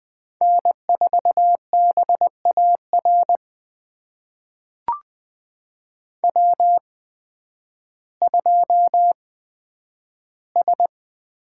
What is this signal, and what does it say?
Morse code, heard as a keyed tone, about 20 wpm, spaced as for 11 wpm, 700 Hz. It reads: N4BAR W2S
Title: USA Call Signs - 5 Characters - 8x Character Spacing - 20wpm